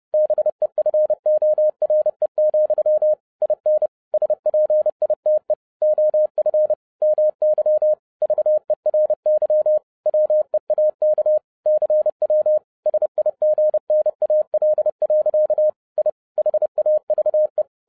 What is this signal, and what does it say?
Morse code, heard as a keyed tone, about 30 wpm, 610 Hz. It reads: BEFORE, IN SPITE OF MY VERY WEAK CW SIGNAL. I HAVE
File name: 2011-07-mp3 / n40 EXCITING QRP CW OPERATION 30wpm